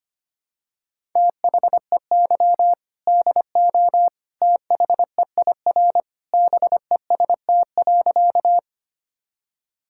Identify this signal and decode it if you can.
Morse code, heard as a keyed tone, about 25 wpm, 700 Hz. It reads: THEY DO THEIR BEST.
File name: practice / 25wpm / Sentences with 2-Word Phrases - 25wpm